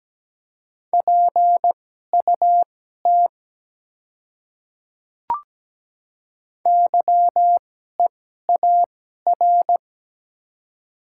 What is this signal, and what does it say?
Morse code, heard as a keyed tone, about 17 wpm, spaced as for 12 wpm, 700 Hz. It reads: PUT YEAR
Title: 2x Character Spacing - Top 200 Words - 17wpm